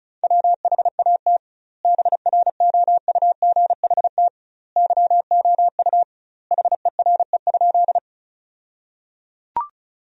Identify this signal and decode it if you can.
Morse code, heard as a keyed tone, about 34 wpm, 700 Hz. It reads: WHAT BROUGHT YOU HERE?
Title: 2x Word-Spacing - Sentences from Top 500 Words - 35wpm